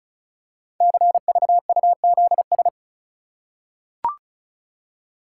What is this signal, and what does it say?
Morse code, heard as a keyed tone, about 36 wpm, 690 Hz. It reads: CVUZS
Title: Code Groups - 5 Letters - 35wpm